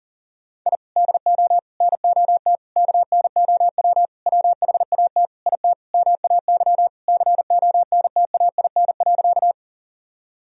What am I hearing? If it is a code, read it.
Morse code, heard as a keyed tone, about 40 wpm, 700 Hz. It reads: I DO NOT KNOW WHAT IT MAY CONTAIN.
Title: Sentences from Top 1000 Words - Review Part 2 - 40wpm